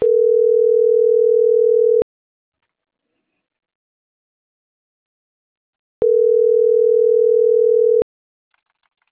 Ringback tone (North America), 440 + 480 Hz.